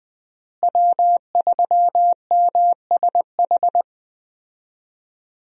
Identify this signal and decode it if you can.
Morse code, heard as a keyed tone, about 20 wpm, 700 Hz. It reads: W3MSH
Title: USA Call Signs - 5 Characters - Repeated 3x - 20wpm